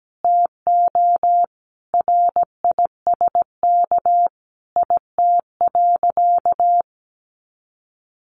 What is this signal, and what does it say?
Morse code, heard as a keyed tone, about 17 wpm, 700 Hz. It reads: TO RISK IT.
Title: Sentences from Top 1500 Words 17wpm